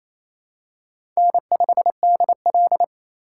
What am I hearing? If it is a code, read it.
Morse code, heard as a keyed tone, about 28 wpm, 700 Hz. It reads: N5DL
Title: USA Call Signs - 3x Character Spacing - 28wpm